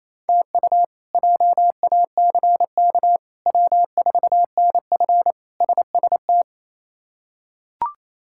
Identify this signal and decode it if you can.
Morse code, heard as a keyed tone, about 28 wpm, 700 Hz. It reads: TU JACK W4NF SST
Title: K1USN SST QSO Practice - 28wpm